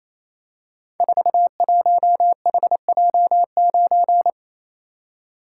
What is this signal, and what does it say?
Morse code, heard as a keyed tone, about 28 wpm, 700 Hz. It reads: 41HJ9
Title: Code Groups - 5 Letter:Numbers - 28wpm